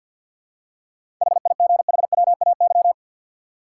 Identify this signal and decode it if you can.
Morse code, heard as a keyed tone, about 50 wpm, 700 Hz. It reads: HIGHWAY
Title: Seven Letter Words - 50wpm